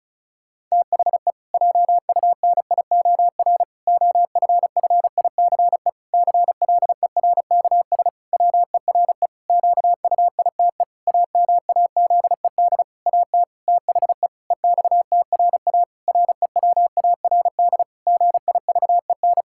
Morse code, heard as a keyed tone, about 36 wpm, 700 Hz. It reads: THE JUNIOR OFFICE CLERKS WERE QUITE AMAZED AT THE EXTRA REWARD GIVEN
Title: Pangram Challenge v2 - 35wpm